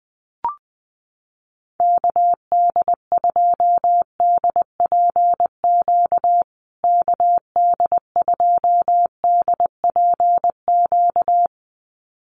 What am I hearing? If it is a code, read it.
Morse code, heard as a keyed tone, about 20 wpm, 700 Hz. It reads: KD2DPQ KD2DPQ
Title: USA Call Signs Repeated 2x 20wpm